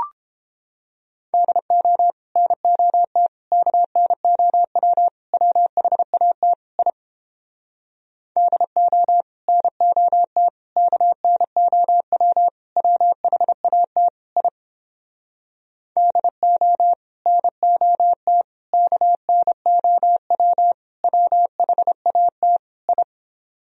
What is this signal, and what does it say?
Morse code, heard as a keyed tone, about 29 wpm, 700 Hz. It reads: DO NOT KNOW WHAT I DO NOT KNOW WHAT I DO NOT KNOW WHAT I